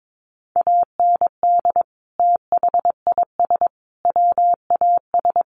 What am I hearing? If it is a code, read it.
Morse code, heard as a keyed tone, about 22 wpm, 700 Hz. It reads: AND THIS WAS
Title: Sentences from Top 1100 Words 22wpm